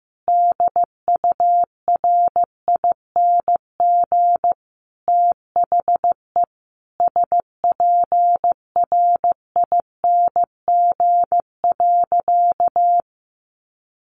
Morse code, heard as a keyed tone, about 15 wpm, 700 Hz. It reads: DURING THE SPRING.
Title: Sentences from Top 1200 Words 15wpm